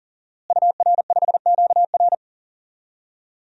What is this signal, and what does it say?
Morse code, heard as a keyed tone, about 37 wpm, 700 Hz. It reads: UR5QR
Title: DXCC-100 Call Signs - 5 Characters - Repeated 3x - 40wpm